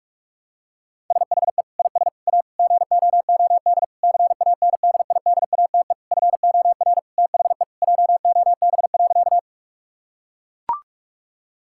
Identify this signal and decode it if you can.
Morse code, heard as a keyed tone, about 41 wpm, 700 Hz. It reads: SHE IS A GOOD CANDIDATE FOR THE JOB.
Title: Sentences from Top 1600 Words 45wpm